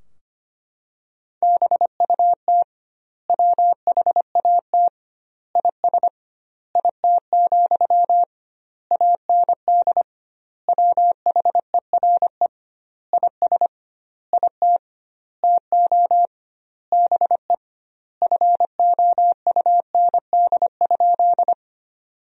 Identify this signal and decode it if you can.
Morse code, heard as a keyed tone, about 25 wpm, 700 Hz. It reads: BUT WHAT IS IT, AND WHERE IS IT TO BE FOUND?
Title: 2x Word-Spacing - Sentences from Top 300 Words - 25wpm